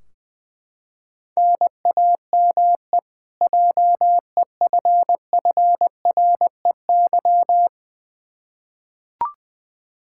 Morse code, heard as a keyed tone, about 20 wpm, 700 Hz. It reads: NAME JEFFREY
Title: Practice QSO - Exchange 1 - Beginning - 20wpm